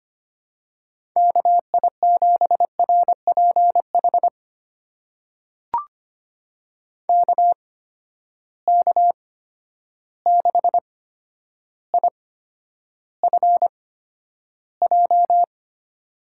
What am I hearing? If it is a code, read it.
Morse code, heard as a keyed tone, about 25 wpm, spaced as for 11 wpm, 700 Hz. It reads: KI7RPH KK6IFJ
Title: USA Call Signs - 6 Characters - 8x Character Spacing - 25wpm